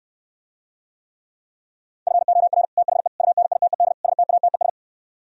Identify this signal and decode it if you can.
Morse code, heard as a keyed tone, about 68 wpm, 700 Hz. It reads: FOR THE UNITED STATES